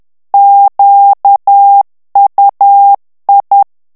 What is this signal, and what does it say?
Morse code, heard as a keyed tone, about 10 wpm, 800 Hz. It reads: QUI
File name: FRANCAIS TEXTE CLAIR 2 VITESSE 10 MOTS MINUTES